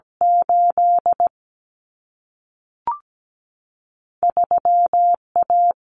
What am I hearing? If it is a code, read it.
Morse code, heard as a keyed tone, about 17 wpm, 700 Hz. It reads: 8 3A
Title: Common Cut Numbers - 2-Digit 17wpm